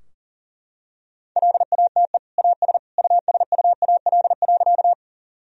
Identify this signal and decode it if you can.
Morse code, heard as a keyed tone, about 40 wpm, 700 Hz. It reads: LATE AS USUAL.
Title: Sentences from Top 1000 Words - Review Part 2 - 40wpm